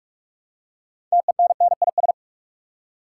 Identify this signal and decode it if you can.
Morse code, heard as a keyed tone, about 44 wpm, 700 Hz. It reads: TENNIS
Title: Six Letter Words - 45wpm